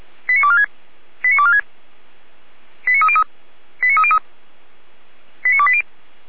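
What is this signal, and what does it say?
Five-tone selective call (ZVEI), heard as quick runs of five tones, tones of about 70 ms. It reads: 89246 89246 89303 89303 89280
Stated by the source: ZVEI five-tone selective call received by SDR, recorded on the Feuerwehr Orlamünde server.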